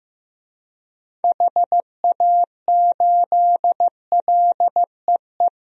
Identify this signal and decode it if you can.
Morse code, heard as a keyed tone, about 15 wpm, 700 Hz. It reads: HA8LEE
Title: DXCC-100 Call Signs - 6 Characters - Repeated 2x - 15wpm